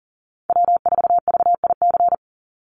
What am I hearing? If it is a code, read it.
Morse code, heard as a keyed tone, about 40 wpm, 700 Hz. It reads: W4VIC
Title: Call Signs Repeated 2x - 40wpm